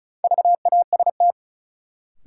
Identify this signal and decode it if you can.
Morse code, heard as a keyed tone, about 34 wpm, 700 Hz. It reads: VAST
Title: Sentences from Top 1300 Words - New Word Drill Alt - 35wpm